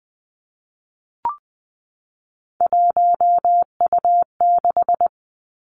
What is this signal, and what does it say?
Morse code, heard as a keyed tone, about 20 wpm, 700 Hz. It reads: 1U6
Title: Three Challenging Letter-Number Combinations 20wpm